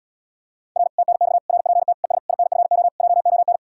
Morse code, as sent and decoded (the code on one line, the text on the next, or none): .-. / - -. -..- / .-. .--. - / . ... / .. -. ..-. --- / .--- --- -.
R TNX RPT ES INFO JON